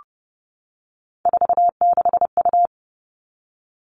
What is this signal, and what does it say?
Morse code, heard as a keyed tone, about 30 wpm, 700 Hz. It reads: 46U
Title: All Cut Numbers - 3-Digit 30wpm